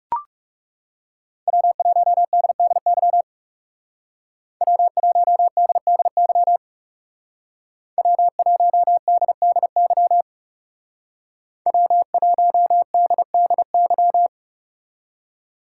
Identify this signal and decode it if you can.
Morse code, heard as a keyed tone, about 36 wpm, 700 Hz. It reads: W1DDY W1DDY W1DDY W1DDY
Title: Speed Racing - USA 1x3 Call Signs 30wpm